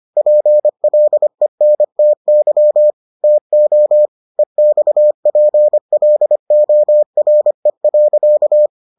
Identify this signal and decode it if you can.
Morse code, heard as a keyed tone, about 25 wpm, 600 Hz. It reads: PLENTY TO EXPLORE.